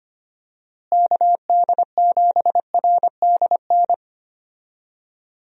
Morse code, heard as a keyed tone, about 25 wpm, 700 Hz. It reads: KD7RDN
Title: USA Call Signs - 6 Characters - Repeated 3x - 25wpm